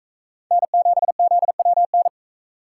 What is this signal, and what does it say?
Morse code, heard as a keyed tone, about 42 wpm, 700 Hz. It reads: N7ZWN